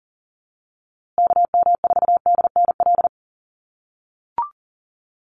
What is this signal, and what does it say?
Morse code, heard as a keyed tone, about 37 wpm, 700 Hz. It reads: KM4DNL
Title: USA Call Signs Repeated 3x 40wpm